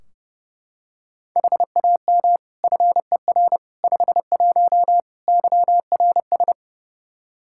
Morse code, heard as a keyed tone, about 30 wpm, 700 Hz. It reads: HAM FER 51 YRS
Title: Practice QSO - Exchange 3 - More Info - 30wpm